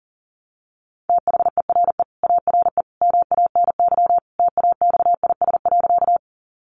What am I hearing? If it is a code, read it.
Morse code, heard as a keyed tone, about 37 wpm, 700 Hz. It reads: THERE ARE MANY TAXIS.
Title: Easy Phrases 40wpm